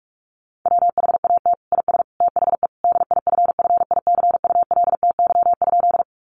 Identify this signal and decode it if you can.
Morse code, heard as a keyed tone, about 44 wpm, 700 Hz. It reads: WHAT IS THE DIFFICULTY?
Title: Sentences from Top 1500 Words 45wpm